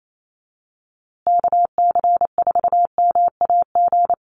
Morse code, heard as a keyed tone, about 28 wpm, 700 Hz. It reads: KC4MAG
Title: USA Call Signs Repeated 3x 28wpm